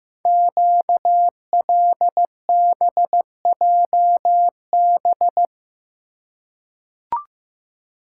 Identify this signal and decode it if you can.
Morse code, heard as a keyed tone, about 15 wpm, 700 Hz. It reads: QLBJB
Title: Code Groups - 5 Letters - 15wpm